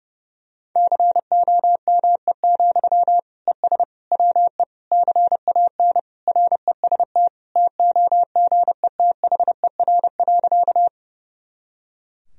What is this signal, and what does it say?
Morse code, heard as a keyed tone, about 32 wpm, 700 Hz. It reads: COME, ES WE CAN REST TOGETHER.
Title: Sentences from Top 500 Words and Abbreviations 30wpm